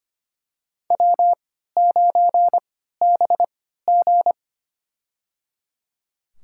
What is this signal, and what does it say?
Morse code, heard as a keyed tone, about 25 wpm, spaced as for 14 wpm, 700 Hz. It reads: W9BG
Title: USA Call Signs - 4 Characters - 3x Character Spacing - 25wpm